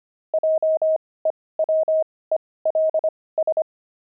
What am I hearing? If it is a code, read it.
Morse code, heard as a keyed tone, about 25 wpm, spaced as for 18 wpm, 630 Hz. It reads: JEWELS
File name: pangram-fs-18-cs-25-grandpa